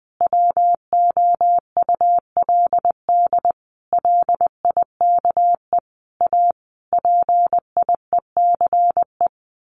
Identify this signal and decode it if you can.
Morse code, heard as a keyed tone, about 20 wpm, 700 Hz. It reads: WOULD LIKE A PIECE
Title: Sentences from Top 1100 Words 20wpm